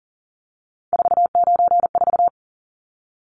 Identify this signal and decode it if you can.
Morse code, heard as a keyed tone, about 40 wpm, 700 Hz. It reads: 494
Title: Numbers - 3-Digit 40wpm